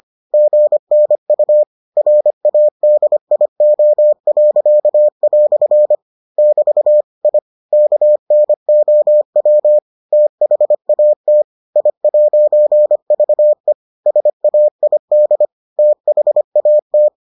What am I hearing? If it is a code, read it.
Morse code, heard as a keyed tone, about 25 wpm, 600 Hz. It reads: GNU RADIO." <BT> I KNOW THAT I'VE SAID THAT